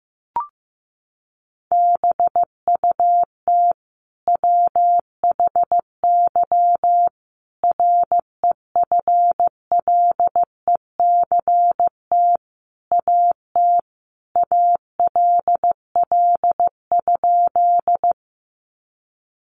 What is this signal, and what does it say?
Morse code, heard as a keyed tone, about 15 wpm, 700 Hz. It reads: BUT WHY REFLECT AT ALL?
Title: Sentences from Top 1500 Words 15wpm